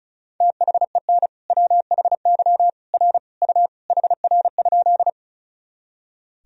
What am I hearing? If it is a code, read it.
Morse code, heard as a keyed tone, about 36 wpm, 700 Hz. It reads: THEN WHY R U HR?
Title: Sentences from Top 200 Words and Abbreviations 35wpm